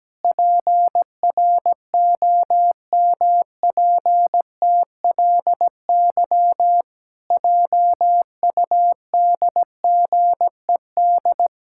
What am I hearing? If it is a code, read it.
Morse code, heard as a keyed tone, about 17 wpm, 700 Hz. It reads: PROMPTLY JUDGED
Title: Pangram Challenge v2 - 17wpm